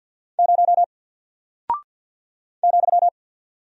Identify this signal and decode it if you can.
Morse code, heard as a keyed tone, about 50 wpm, 700 Hz. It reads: 0 ,
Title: Single Character (Letter:Number:Punctuation:Prosign) Alt 50wpm